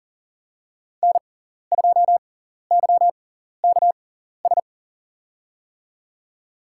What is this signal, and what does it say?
Morse code, heard as a keyed tone, about 40 wpm, spaced as for 14 wpm, 700 Hz. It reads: N2YKS
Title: USA Call Signs - 5 Characters - 6x Character Spacing - 40wpm